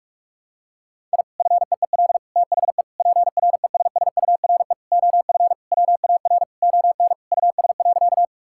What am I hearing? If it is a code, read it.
Morse code, heard as a keyed tone, about 46 wpm, 700 Hz. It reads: I FEEL THE PRESSURE OF WAR ON US.
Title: Sentences from Top 1000 Words - Review Part 6 - 45wpm